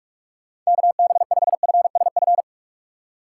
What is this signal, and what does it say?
Morse code, heard as a keyed tone, about 46 wpm, 700 Hz. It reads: KB5FSF